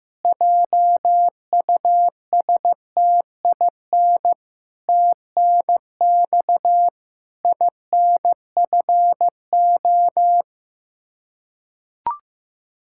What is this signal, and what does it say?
Morse code, heard as a keyed tone, about 15 wpm, 700 Hz. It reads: JUSTIN TNX INFO
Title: Practice QSO - Exchange 3 - More Info - 15wpm